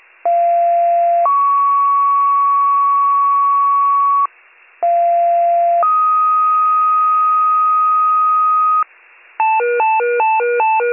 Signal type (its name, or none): two-tone page